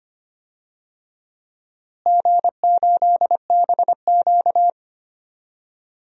Morse code, heard as a keyed tone, about 25 wpm, 700 Hz. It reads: G8BQ
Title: DXCC-100 Call Signs - 4 Characters - Repeated 2x - 25wpm